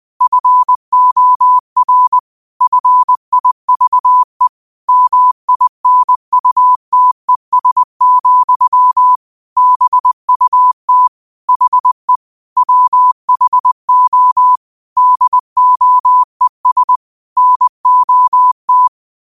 Morse code, heard as a keyed tone, about 20 wpm, 1000 Hz. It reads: FOR FIVE MINUTES, BUT HE WHO DOES NOT
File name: qotdfiles / 20wpm